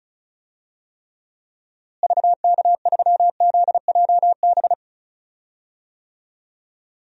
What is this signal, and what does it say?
Morse code, heard as a keyed tone, about 34 wpm, 700 Hz. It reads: VK3ZJB
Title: DXCC-100 Call Signs - 6 Characters - Repeated 2x - 35wpm